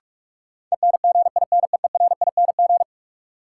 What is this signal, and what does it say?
Morse code, heard as a keyed tone, about 43 wpm, 700 Hz. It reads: ENGINEERING
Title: Sentences from Top 1500 Words - New Word Drill - 45wpm